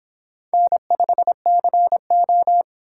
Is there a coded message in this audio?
Yes: Morse code from 0.5 to 2.6 s, heard as a keyed tone, about 26 wpm, 700 Hz. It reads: N5CO